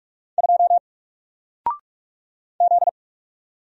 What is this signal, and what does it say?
Morse code, heard as a keyed tone, about 44 wpm, 700 Hz. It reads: 2 Z
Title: Single Letter-Number Alt 45wpm